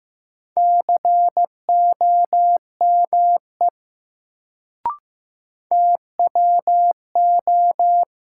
Morse code, heard as a keyed tone, about 15 wpm, 700 Hz. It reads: COME TWO
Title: Sentences from Top 100 Words - New Word Drill - 15wpm